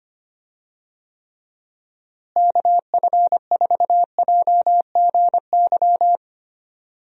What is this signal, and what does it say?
Morse code, heard as a keyed tone, about 25 wpm, 700 Hz. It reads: KF4JGY
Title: USA Call Signs - 6 Characters - Repeated 2x - 25wpm